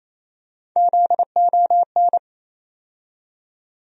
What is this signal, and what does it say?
Morse code, heard as a keyed tone, about 28 wpm, 700 Hz. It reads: ZON
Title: USA Call Sign Postfixes - 3 Letters 28wpm